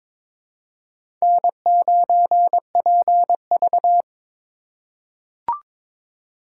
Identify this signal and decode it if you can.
Morse code, heard as a keyed tone, about 22 wpm, 700 Hz. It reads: N9PV
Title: USA Call Signs - 4 Characters - 4x Character Spacing - 22wpm